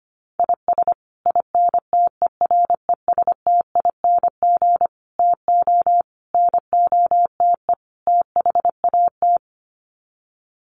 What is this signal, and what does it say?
Morse code, heard as a keyed tone, about 25 wpm, 700 Hz. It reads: IS INTERESTING TO NOTE THAT
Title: Sets of 5 Words Encore2 25wpm